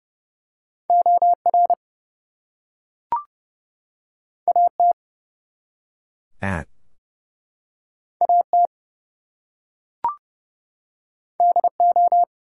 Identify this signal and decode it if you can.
Morse code, heard as a keyed tone, about 30 wpm, 700 Hz. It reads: OR AT AT DO
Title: Two Letter Words - 30wpm